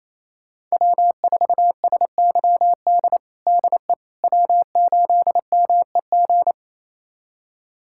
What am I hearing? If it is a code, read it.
Morse code, heard as a keyed tone, about 28 wpm, 700 Hz. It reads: W4SYD DE W8MEG
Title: USA Callsign DE Callsign - 5 Characters - 28wpm